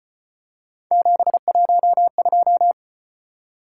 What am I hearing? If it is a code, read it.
Morse code, heard as a keyed tone, about 33 wpm, 700 Hz. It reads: 712